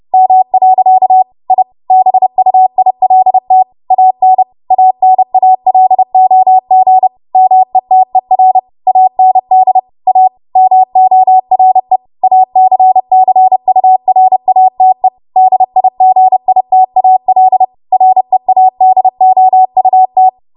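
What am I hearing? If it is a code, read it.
Morse code, heard as a keyed tone, about 30 wpm, 750 Hz. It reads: M. I BUILT AN ANALOG METER AND A MORE ACCURATE DIGITAL READOUT
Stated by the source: W1AW Web Code Practice Files, 30 WPM run from ARRL; text from April 2023 QST.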